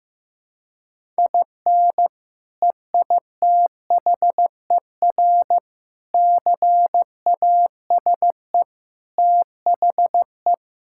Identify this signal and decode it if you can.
Morse code, heard as a keyed tone, about 15 wpm, 700 Hz. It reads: IN EITHER CASE THE
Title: Sentences from Top 1000 Words - Review Part 6 - 15wpm